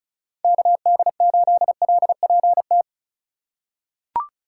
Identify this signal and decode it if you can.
Morse code, heard as a keyed tone, about 36 wpm, 700 Hz. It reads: KD8LPT
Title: USA Call Signs - 6 Characters - 3x Character Spacing - 35wpm